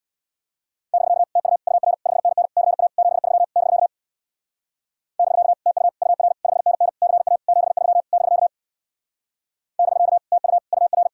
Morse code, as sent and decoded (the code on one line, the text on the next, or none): -...-.- / - ..- / ..- .-. / ..... -. -. / -.-. .- / --... ...-- / -...-.- / -...-.- / - ..- / ..- .-. / ..... -. -. / -.-. .- / --... ...-- / -...-.- / -...-.- / - ..- / ..- .-.
<BK> TU UR 5NN CA 73 <BK> <BK> TU UR 5NN CA 73 <BK> <BK> TU UR